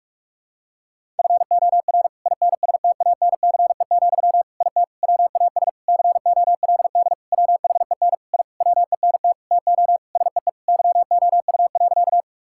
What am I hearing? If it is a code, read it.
Morse code, heard as a keyed tone, about 46 wpm, 700 Hz. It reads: FOR INSTANCE, IT WAS COLD WHEN I WENT TO SEE YOU.
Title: Sentences from Top 1000 Words - Review Part 5 - 45wpm